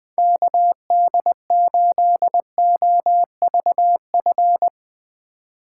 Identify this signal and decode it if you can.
Morse code, heard as a keyed tone, about 21 wpm, 700 Hz. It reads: KD8OVF